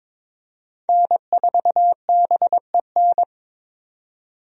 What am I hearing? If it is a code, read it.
Morse code, heard as a keyed tone, about 22 wpm, 700 Hz. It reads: N4BEN
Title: USA Call Signs - 5 Characters - 22wpm